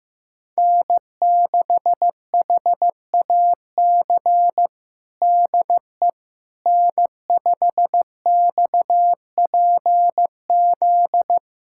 Morse code, heard as a keyed tone, about 15 wpm, 700 Hz. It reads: N6HAC DE N5XPZ
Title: USA Callsign DE Callsign - 5 Characters - 15wpm